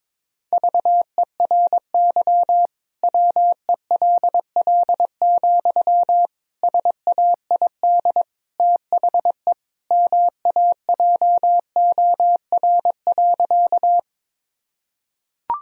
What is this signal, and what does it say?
Morse code, heard as a keyed tone, about 22 wpm, 700 Hz. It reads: VERY WELL, SAID THE MAJOR.
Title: Sentences from Top 1000 Words - Review Part 4 - 22wpm